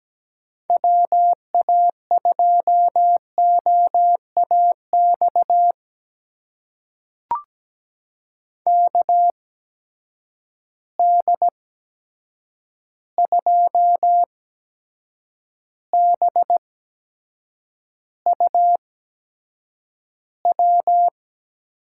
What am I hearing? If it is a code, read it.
Morse code, heard as a keyed tone, about 17 wpm, spaced as for 7 wpm, 700 Hz. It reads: WA2OAX KD2BUW
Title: USA Call Signs - 6 Characters - 8x Character Spacing - 17wpm